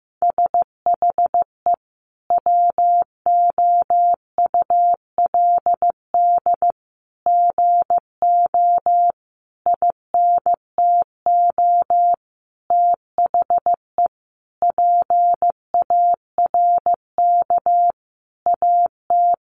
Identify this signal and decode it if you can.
Morse code, heard as a keyed tone, about 15 wpm, 700 Hz. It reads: SHE WOULD GO INTO THE PARK AT
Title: Sentences from Top 1300 Words 15wpm